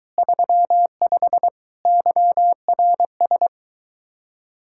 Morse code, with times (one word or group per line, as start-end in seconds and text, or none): 0.2-1.5 35
1.9-3.5 YRS